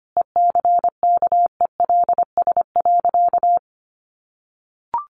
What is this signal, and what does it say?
Morse code, heard as a keyed tone, about 25 wpm, 700 Hz. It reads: ECKELS.
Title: A Sound Of Thunder 25wpm